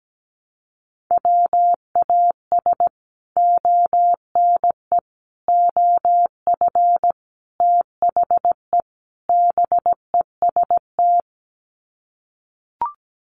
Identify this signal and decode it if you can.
Morse code, heard as a keyed tone, about 17 wpm, 700 Hz. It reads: WAS ONE OF THE BEST
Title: Sets of 5 Words Encore2 17wpm